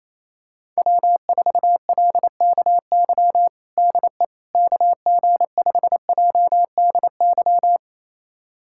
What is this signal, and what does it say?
Morse code, heard as a keyed tone, about 28 wpm, 700 Hz. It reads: W4LKY DE KG5JDY
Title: USA Callsign DE Callsign 28wpm